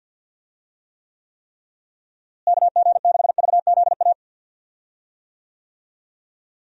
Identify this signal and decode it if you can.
Morse code, heard as a keyed tone, about 50 wpm, 700 Hz. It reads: KG6VZA
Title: USA Call Signs - 6 Characters - Repeated 3x - 50wpm